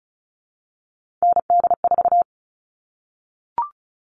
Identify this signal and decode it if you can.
Morse code, heard as a keyed tone, about 34 wpm, 700 Hz. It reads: ND4
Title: All Cut Numbers - 3-Digit 35wpm